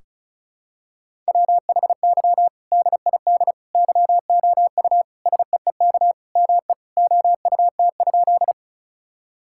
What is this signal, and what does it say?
Morse code, heard as a keyed tone, about 36 wpm, 700 Hz. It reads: WHY DID YOU SEEK ME OUT?
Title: Sentences from Top 1600 Words 35wpm